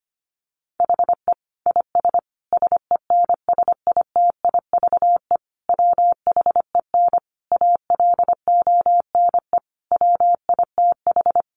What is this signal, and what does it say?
Morse code, heard as a keyed tone, about 25 wpm, 700 Hz. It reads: HE IS SENSITIVE WHEN ALONE WITH
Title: Sentences from Top 1400 Words 25wpm